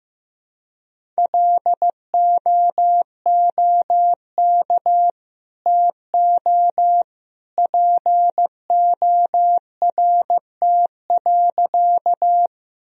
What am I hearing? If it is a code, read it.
Morse code, heard as a keyed tone, about 15 wpm, 700 Hz. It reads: LOOK TO PORT.
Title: Sentences from Top 1000 Words - Review Part 1 - 15wpm